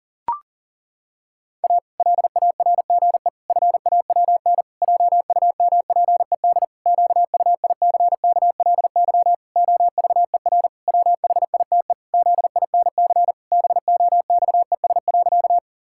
Morse code, heard as a keyed tone, about 40 wpm, 700 Hz. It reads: A LARGE FAWN JUMPED QUICKLY OVER WHITE ZINC BOXES.
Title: Pangram Challenge v2 - 40wpm